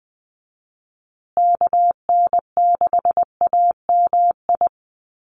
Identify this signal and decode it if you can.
Morse code, heard as a keyed tone, about 20 wpm, 700 Hz. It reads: KN6AMI
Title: USA Call Signs Repeated 3x 20wpm